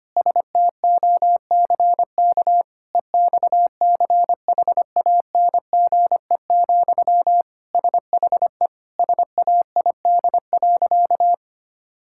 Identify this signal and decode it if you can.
Morse code, heard as a keyed tone, about 25 wpm, 700 Hz. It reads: STOCK EXCHANGE, SHE SAID.
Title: Sentences from Top 1000 Words - Review Part 6 - 25wpm